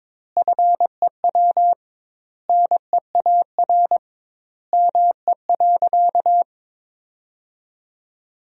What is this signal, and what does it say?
Morse code, heard as a keyed tone, about 21 wpm, 700 Hz. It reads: FEW NEAR ME.
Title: 2x Word-Spacing - Sentences from Top 300 Words - 22wpm